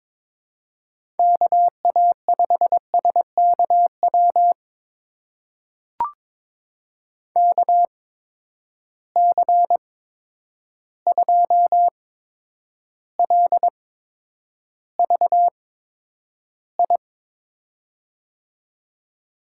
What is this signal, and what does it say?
Morse code, heard as a keyed tone, about 22 wpm, spaced as for 9 wpm, 700 Hz. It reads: KA5SKW KC2LVI
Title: USA Call Signs - 6 Characters - 8x Character Spacing - 22wpm